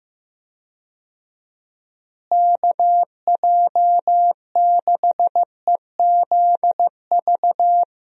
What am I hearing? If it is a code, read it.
Morse code, heard as a keyed tone, about 15 wpm, 700 Hz. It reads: KJ6EZV
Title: USA Call Signs - 6 Characters - Repeated 2x - 15wpm